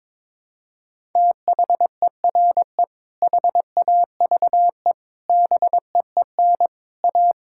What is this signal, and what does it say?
Morse code, heard as a keyed tone, about 22 wpm, 700 Hz. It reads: THERE HAVE BEEN A
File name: practice / 22wpm / Sets of 6 Words - 22wpm